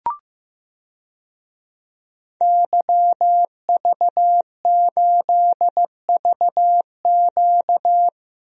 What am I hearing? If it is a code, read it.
Morse code, heard as a keyed tone, about 15 wpm, 700 Hz. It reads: YV8VQ
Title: DXCC-100 Call Signs - 5 Characters - Repeated 2x - 15wpm